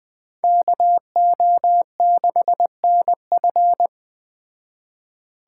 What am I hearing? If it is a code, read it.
Morse code, heard as a keyed tone, about 20 wpm, 700 Hz. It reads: KO6NF